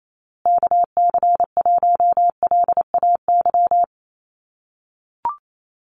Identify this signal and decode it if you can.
Morse code, heard as a keyed tone, about 28 wpm, 700 Hz. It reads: KC1LAY
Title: USA Call Signs Repeated 2x 28wpm